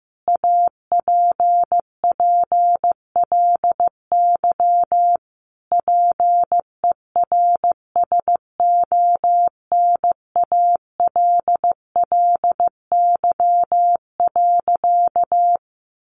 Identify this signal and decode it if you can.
Morse code, heard as a keyed tone, about 15 wpm, 700 Hz. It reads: APPLY PERSONALLY.
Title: Sentences from Top 1400 Words 15wpm